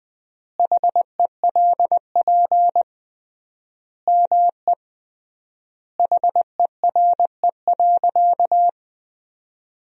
Morse code, heard as a keyed tone, about 20 wpm, 700 Hz. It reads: HELP ME HERE.
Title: 3x Word-Spacing - Sentences from Top 200 Words - 20wpm